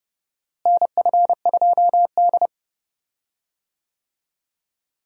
Morse code, heard as a keyed tone, about 30 wpm, 700 Hz. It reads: NF2D
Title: USA Call Signs - 4 Characters - Repeated 3x - 30wpm